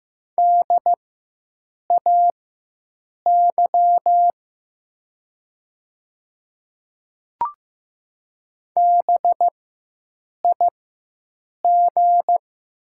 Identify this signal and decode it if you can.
Morse code, heard as a keyed tone, about 15 wpm, spaced as for 7 wpm, 700 Hz. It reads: DAY BIG
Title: 4x Character Spacing - Top 200 Words - 15wpm